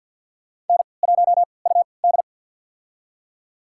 Morse code, heard as a keyed tone, about 52 wpm, spaced as for 29 wpm, 700 Hz. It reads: N1UD
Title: USA Call Signs - 4 Characters - 3x Character Spacing - 50wpm